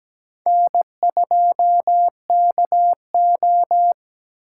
Morse code, heard as a keyed tone, about 20 wpm, 700 Hz. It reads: N2KO
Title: Speed Racing - USA 1x2 Call Signs 17wpm